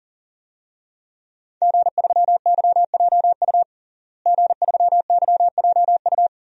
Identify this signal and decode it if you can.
Morse code, heard as a keyed tone, about 37 wpm, 700 Hz. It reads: G3YJU G3YJU
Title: DXCC-100 Call Signs - 5 Characters - Repeated 3x - 40wpm